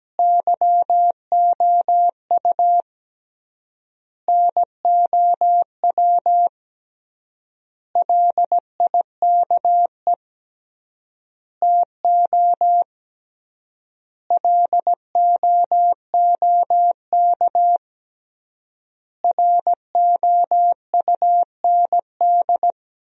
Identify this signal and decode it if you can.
Morse code, heard as a keyed tone, about 17 wpm, 700 Hz. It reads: YOU NOW LIKE TO LOOK ROUND
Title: 3x Word-Spacing - Sentences from Top 200 Words - 17wpm